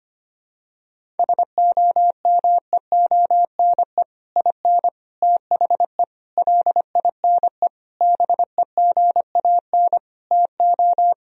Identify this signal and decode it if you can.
Morse code, heard as a keyed tone, about 25 wpm, 700 Hz. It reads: SOMEONE IN THE LINE BEGAN TO
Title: Sentences from Top 1000 Words - Review Part 1 - 25wpm